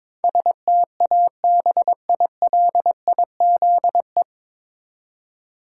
Morse code, heard as a keyed tone, about 22 wpm, 700 Hz. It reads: STABILIZE
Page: Morse Code Ninja practice 22wpm